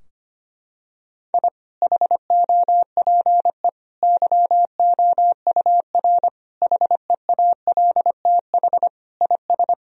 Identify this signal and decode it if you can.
Morse code, heard as a keyed tone, about 25 wpm, 700 Hz. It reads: I HOPE YOUR HEALTH IS
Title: Sentences from Top 1000 Words - Review Part 3 - 25wpm